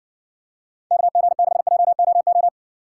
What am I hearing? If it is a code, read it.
Morse code, heard as a keyed tone, about 60 wpm, 700 Hz. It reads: KG6JOO